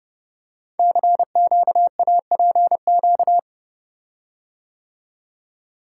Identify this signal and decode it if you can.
Morse code, heard as a keyed tone, about 30 wpm, 700 Hz. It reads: CQAPQ
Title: Code Groups - 5 Letters - 30wpm